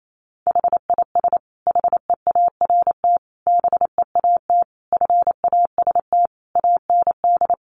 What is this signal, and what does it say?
Morse code, heard as a keyed tone, about 28 wpm, 700 Hz. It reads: HIS HEART BEAT FAST AND
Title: Sentences from Top 1300 Words 28wpm